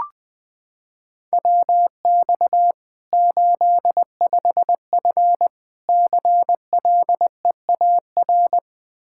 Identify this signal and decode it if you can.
Morse code, heard as a keyed tone, about 20 wpm, 700 Hz. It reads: WX 85F CLEAR
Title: Practice QSO - Exchange 2 - Info - 20wpm